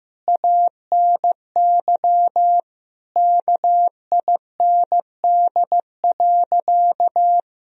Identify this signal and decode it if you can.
Morse code, heard as a keyed tone, about 15 wpm, 700 Hz. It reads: ANY KIND.
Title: Sentences from Top 1000 Words - Review Part 2 - 15wpm